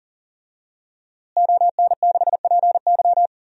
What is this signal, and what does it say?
Morse code, heard as a keyed tone, about 40 wpm, 700 Hz. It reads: ON6PY
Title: DXCC-100 Call Signs - 5 Characters - Repeated 2x - 40wpm